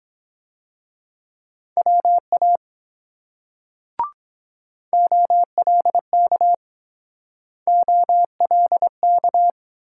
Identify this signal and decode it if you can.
Morse code, heard as a keyed tone, about 23 wpm, 690 Hz. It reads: WA OLK OLK